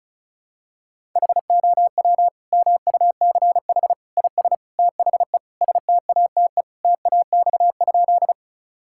Morse code, heard as a keyed tone, about 34 wpm, 700 Hz. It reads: HOW MUCH IS THE STATE TAX?
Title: Sentences from Top 1000 Words - Review Part 5 - 35wpm